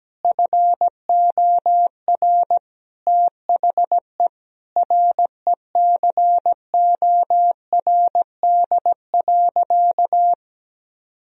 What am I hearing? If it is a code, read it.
Morse code, heard as a keyed tone, about 17 wpm, 700 Hz. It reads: FOR THE RECORD.
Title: Sentences from Top 1000 Words - Review Part 3 - 17wpm